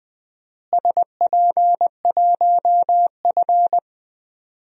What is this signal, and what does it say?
Morse code, heard as a keyed tone, about 20 wpm, 700 Hz. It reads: SP1F
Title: DXCC-100 Call Signs - 4 Characters - Repeated 2x - 20wpm